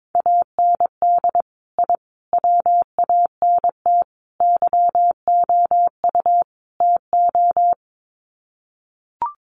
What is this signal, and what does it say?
Morse code, heard as a keyed tone, about 22 wpm, 700 Hz. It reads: AND I WANT YOU TO
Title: Sets of 5 Words Encore 22wpm